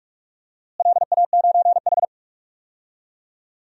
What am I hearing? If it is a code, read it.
Morse code, heard as a keyed tone, about 46 wpm, 700 Hz. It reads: LA9H
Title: DXCC-100 Call Signs - 4 Characters - Repeated 2x - 45wpm